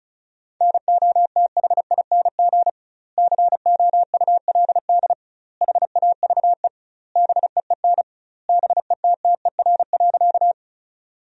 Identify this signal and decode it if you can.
Morse code, heard as a keyed tone, about 34 wpm, 700 Hz. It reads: NOTHING COULD HAVE BEEN BETTER.
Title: 2x Word-Spacing - Sentences from Top 500 Words - 35wpm